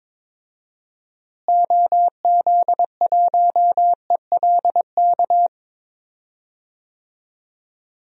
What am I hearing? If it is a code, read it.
Morse code, heard as a keyed tone, about 22 wpm, 700 Hz. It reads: OZ1ELK
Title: DXCC-100 Call Signs - 6 Characters - Repeated 2x - 22wpm